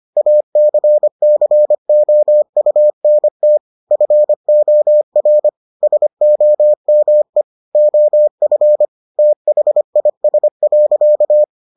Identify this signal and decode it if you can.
Morse code, heard as a keyed tone, about 25 wpm, 600 Hz. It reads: ACCOUNT FOR SOME OF THIS.